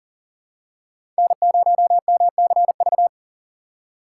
Morse code, heard as a keyed tone, about 40 wpm, 700 Hz. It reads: N0MCV
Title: USA Call Signs - 2x Character Spacing - 40wpm